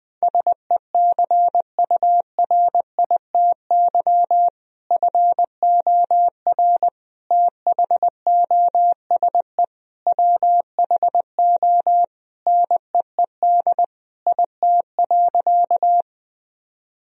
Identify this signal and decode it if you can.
Morse code, heard as a keyed tone, about 21 wpm, 700 Hz. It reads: SECURITY FOR THOSE WHO NEED IT.
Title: Sentences from Top 1000 Words - Review Part 4 - 20wpm